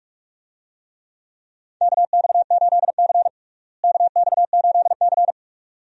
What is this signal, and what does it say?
Morse code, heard as a keyed tone, about 43 wpm, 700 Hz. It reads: KX8C KX8C
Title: USA Call Signs - 4 Characters - Repeated 2x - 45wpm